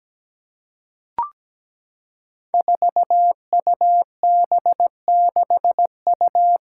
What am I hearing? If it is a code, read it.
Morse code, heard as a keyed tone, about 17 wpm, 700 Hz. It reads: 4UB6U
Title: Five Challenging Letter-Number Combinations 17wpm